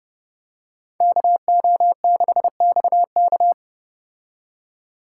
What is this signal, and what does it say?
Morse code, heard as a keyed tone, about 31 wpm, 700 Hz. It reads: KO6XK